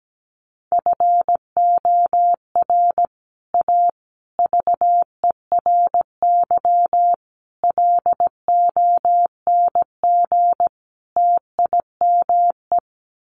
Morse code, heard as a keyed tone, about 17 wpm, 700 Hz. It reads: FOR A VERY LONG TIME
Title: Sets of 5 Words 17wpm